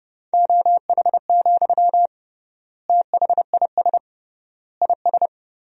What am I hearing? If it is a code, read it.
Morse code, heard as a keyed tone, about 30 wpm, 700 Hz. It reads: OH, THIS IS
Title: 3x Word-Spacing - Sentences from Top 500 Words - 30wpm